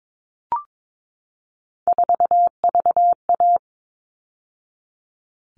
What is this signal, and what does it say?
Morse code, heard as a keyed tone, about 22 wpm, 700 Hz. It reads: 4VA
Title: All Cut Numbers - 3-Digit 22wpm